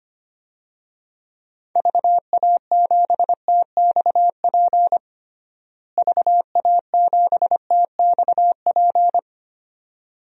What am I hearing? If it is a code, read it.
Morse code, heard as a keyed tone, about 25 wpm, 700 Hz. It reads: VA7TXP VA7TXP
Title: DXCC-100 Call Signs - 6 Characters - Repeated 2x - 25wpm